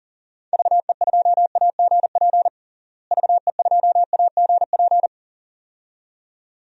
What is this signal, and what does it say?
Morse code, heard as a keyed tone, about 40 wpm, 700 Hz. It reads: VE2AGP VE2AGP
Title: DXCC-100 Call Signs - 6 Characters - Repeated 2x - 40wpm